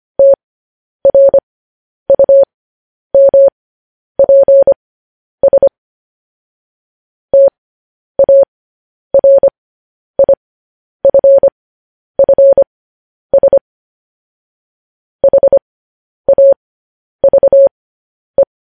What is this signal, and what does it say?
Morse code, heard as a keyed tone, about 25 wpm, spaced as for 10 wpm, 550 Hz. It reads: TRUMPS TARIFFS HAVE
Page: news headlines in morse code at 10 wpm